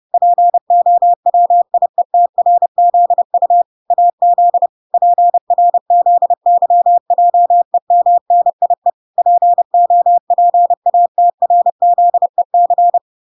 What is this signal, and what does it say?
Morse code, heard as a keyed tone, about 30 wpm, 700 Hz. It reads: POWIETRZU AZ PRZYJEMNIE POPATRZEC